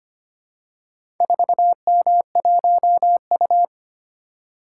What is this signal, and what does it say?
Morse code, heard as a keyed tone, about 25 wpm, 700 Hz. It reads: 4M1U
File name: practice / 25wpm / Four Letter-Number Combinations - 25wpm